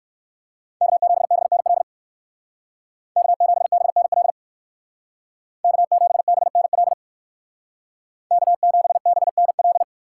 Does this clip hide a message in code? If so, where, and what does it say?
Yes: Morse code from 0.8 to 9.8 s, heard as a keyed tone, about 59 wpm, 700 Hz. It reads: K7BNL K7BNL K7BNL K7BNL